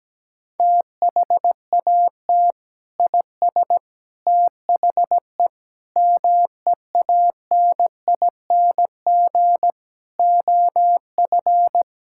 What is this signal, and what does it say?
Morse code, heard as a keyed tone, about 17 wpm, 700 Hz. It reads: THAT IS THE MEANING OF
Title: Sentences from Top 1600 Words 17wpm